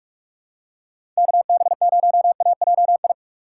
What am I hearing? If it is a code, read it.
Morse code, heard as a keyed tone, about 42 wpm, 700 Hz. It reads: KB0AJI
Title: USA Call Signs - 6 Characters - 3x Character Spacing - 45wpm